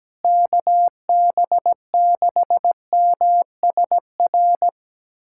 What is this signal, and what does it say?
Morse code, heard as a keyed tone, about 17 wpm, 700 Hz. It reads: KB6MSR
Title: USA Callsign DE Callsign - 6 Characters - 17wpm